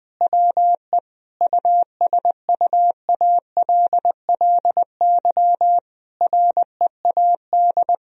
Morse code, heard as a keyed tone, about 20 wpm, 700 Hz. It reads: WE USUALLY READ
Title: Sentences from Top 1000 Words - Review Part 3 - 20wpm